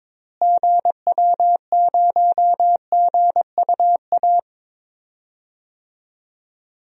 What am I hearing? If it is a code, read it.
Morse code, heard as a keyed tone, about 22 wpm, 700 Hz. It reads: GW0GUA
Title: DXCC-100 Call Signs - 6 Characters - Repeated 2x - 22wpm